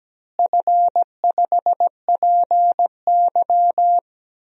Morse code, heard as a keyed tone, about 17 wpm, 700 Hz. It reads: F5PY
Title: DXCC-100 Call Signs - 4 Characters - Repeated 3x - 17wpm